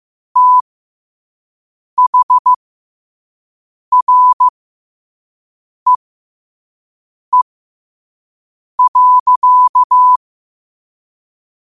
Morse code, heard as a keyed tone, about 15 wpm, spaced as for 5 wpm, 1000 Hz. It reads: THREE.